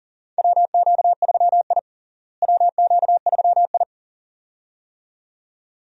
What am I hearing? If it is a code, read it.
Morse code, heard as a keyed tone, about 40 wpm, 700 Hz. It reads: WQ3I WQ3I
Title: USA Call Signs - 4 Characters - Repeated 2x - 40wpm